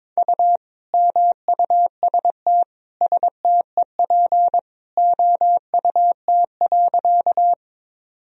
Morse code, heard as a keyed tone, about 22 wpm, 700 Hz. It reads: U MUST STEP OUT.
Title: Sentences from Top 400 Words and Abbreviations 22wpm